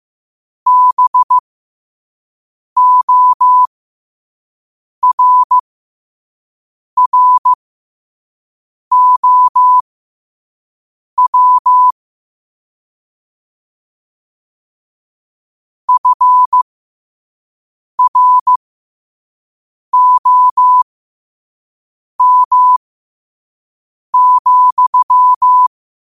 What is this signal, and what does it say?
Morse code, heard as a keyed tone, about 15 wpm, spaced as for 5 wpm, 1000 Hz. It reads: BORROW FROM,
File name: qotdfiles / 5wpm